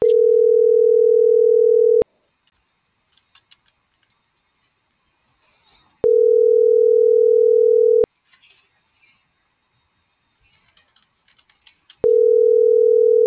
Ringback tone (North America), 440 + 480 Hz.